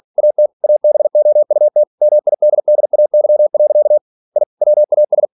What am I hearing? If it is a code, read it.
Morse code, heard as a keyed tone, about 46 wpm, 600 Hz. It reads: AT ABOUT MIDDAY. I WAS
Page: Never 47wpm